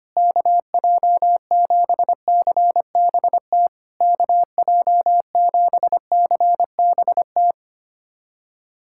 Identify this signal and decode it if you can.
Morse code, heard as a keyed tone, about 25 wpm, 700 Hz. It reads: KJ7CBT KJ7CBT
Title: SOTA QSO Practice 25wpm